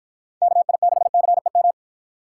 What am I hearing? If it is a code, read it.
Morse code, heard as a keyed tone, about 52 wpm, 700 Hz. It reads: KI6CEM